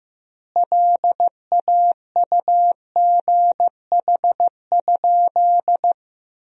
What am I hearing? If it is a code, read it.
Morse code, heard as a keyed tone, about 16 wpm, 700 Hz. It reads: LAUGH?
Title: 2x Word-Spacing - Sentences from Top 500 Words - 15wpm